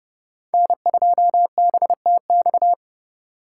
Morse code, heard as a keyed tone, about 31 wpm, 700 Hz. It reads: N2BTX
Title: K1USN SST QSO Practice - 30wpm